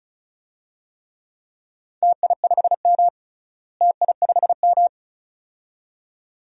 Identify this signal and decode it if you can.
Morse code, heard as a keyed tone, about 36 wpm, 700 Hz. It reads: TI5M TI5M
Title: DXCC-100 Call Signs - 4 Characters - Repeated 2x - 35wpm